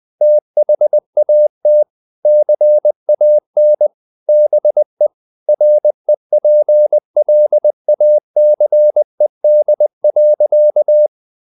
Morse code, heard as a keyed tone, about 20 wpm, 600 Hz. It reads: THAT CAN BE REPLACED.
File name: the old man and the sea / the old man and the sea part 1 20wpm